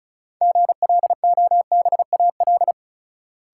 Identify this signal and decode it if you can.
Morse code, heard as a keyed tone, about 36 wpm, 700 Hz. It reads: GLOBAL
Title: Sentences from Top 1200 Words - New Word Drill Alt - 35wpm